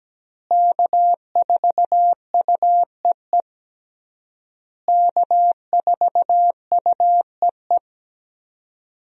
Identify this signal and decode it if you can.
Morse code, heard as a keyed tone, about 17 wpm, 700 Hz. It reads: K4UEE K4UEE
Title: USA Call Signs - 5 Characters - Repeated 3x - 17wpm